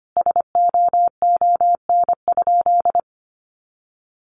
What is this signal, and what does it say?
Morse code, heard as a keyed tone, about 25 wpm, 700 Hz. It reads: SOON?
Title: Sentences from Top 1300 Words 25wpm